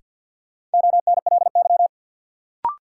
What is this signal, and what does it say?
Morse code, heard as a keyed tone, about 50 wpm, 700 Hz. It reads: ONLY
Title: Sets of 2 Words - New Word Drill - 50wpm